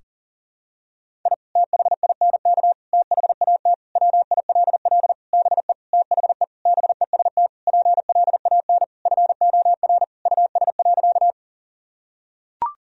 Morse code, heard as a keyed tone, about 37 wpm, 700 Hz. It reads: I THINK THAT WILL BE THE BEST PLAN FOR US.
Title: Sentences from Top 1000 Words - Review Part 2 - 40wpm